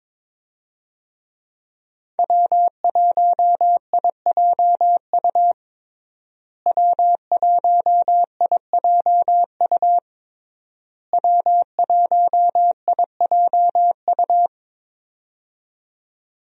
Morse code, heard as a keyed tone, about 22 wpm, 700 Hz. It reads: W1IJU W1IJU W1IJU
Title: USA Call Signs - 5 Characters - Repeated 3x - 22wpm